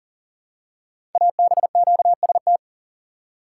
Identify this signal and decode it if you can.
Morse code, heard as a keyed tone, about 39 wpm, 700 Hz. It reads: ABQST
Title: Code Groups - 5 Letters - 40wpm